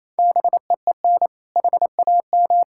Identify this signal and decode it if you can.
Morse code, heard as a keyed tone, about 27 wpm, 700 Hz. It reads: BEEN HAM
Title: K1USN SST QSO Practice 28wpm